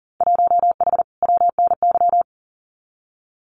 Morse code, heard as a keyed tone, about 38 wpm, 700 Hz. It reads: 1H WNY
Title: Winter Field Day Exchanges 40wpm